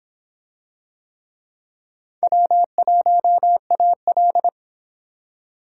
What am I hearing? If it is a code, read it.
Morse code, heard as a keyed tone, about 26 wpm, 700 Hz. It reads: W1AL